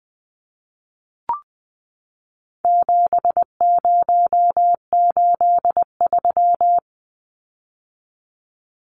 Morse code, heard as a keyed tone, about 20 wpm, 700 Hz. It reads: 7083
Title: Numbers - 4-Digit 20wpm